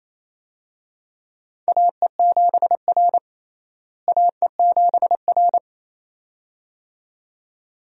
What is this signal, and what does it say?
Morse code, heard as a keyed tone, about 28 wpm, 700 Hz. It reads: AE7R AE7R
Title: USA Call Signs - 4 Characters - Repeated 2x - 28wpm